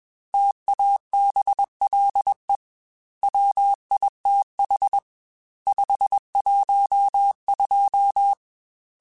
Morse code, heard as a keyed tone, about 21 wpm, 800 Hz. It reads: TABLE WITH 512
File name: Technical Subjects / Technical Subjects-mp3s / AND Gate @20